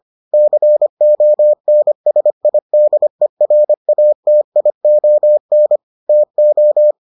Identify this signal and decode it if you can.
Morse code, heard as a keyed tone, about 25 wpm, 600 Hz. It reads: CONSIDERATION TO